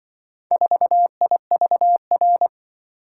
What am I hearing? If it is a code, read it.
Morse code, heard as a keyed tone, about 24 wpm, 700 Hz. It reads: 4IVR
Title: Speed-Racing - Four Letter-Number Combinations - 20wpm